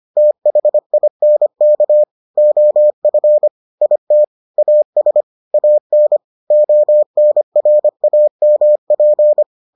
Morse code, heard as a keyed tone, about 25 wpm, 600 Hz. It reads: THINK OF IT AS AN ONRAMP